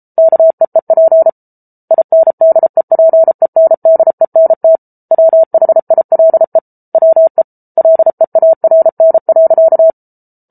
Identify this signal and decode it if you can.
Morse code, heard as a keyed tone, about 34 wpm, 650 Hz. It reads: KEEP INDEPENDENT WHILE WE LEARN.